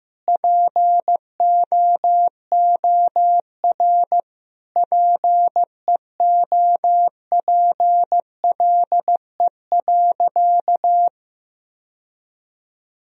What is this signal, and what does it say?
Morse code, heard as a keyed tone, about 15 wpm, 700 Hz. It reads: POOR PEOPLE.
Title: Sentences from Top 1000 Words - Review Part 1 - 15wpm